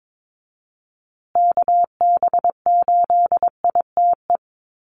Morse code, heard as a keyed tone, about 22 wpm, 700 Hz. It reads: KB8ITE
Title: USA Call Signs Repeated 2x 22wpm